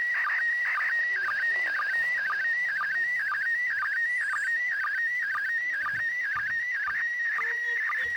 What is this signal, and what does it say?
SSTV (slow-scan television), PD 120, with speech.
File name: 08-ARISS-SSTV-Expedition-63-ARISS-Series-16-Satellites